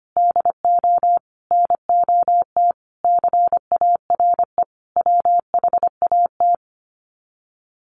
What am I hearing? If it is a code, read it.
Morse code, heard as a keyed tone, about 25 wpm, 700 Hz. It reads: DO NOT CARE WHAT
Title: Sets of 5 Words 25wpm